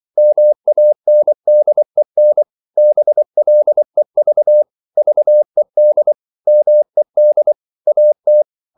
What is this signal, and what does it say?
Morse code, heard as a keyed tone, about 24 wpm, 600 Hz. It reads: MANDEN BLEV VED MED AT
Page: Ferie 24 wpm